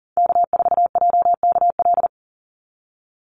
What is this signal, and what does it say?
Morse code, heard as a keyed tone, about 40 wpm, 700 Hz. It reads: K4JKL
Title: Call Signs Repeated 3x - 40wpm